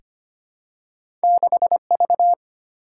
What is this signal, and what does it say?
Morse code, heard as a keyed tone, about 25 wpm, 700 Hz. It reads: 6V
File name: Two Letter-Number Combinations - 25wpm